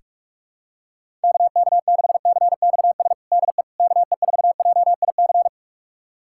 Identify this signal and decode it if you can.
Morse code, heard as a keyed tone, about 46 wpm, 700 Hz. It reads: KK6CXS DE KE4JIC
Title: USA Callsign DE Callsign - 6 Characters - 45wpm